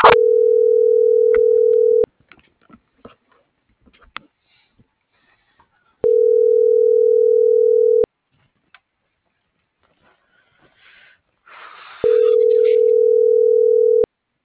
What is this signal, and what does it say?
Ringback tone (North America), 440 + 480 Hz.